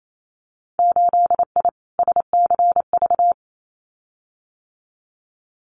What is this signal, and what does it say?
Morse code, heard as a keyed tone, about 28 wpm, 700 Hz. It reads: 8I SCV
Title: Winter Field Day Exchanges Encore 28wpm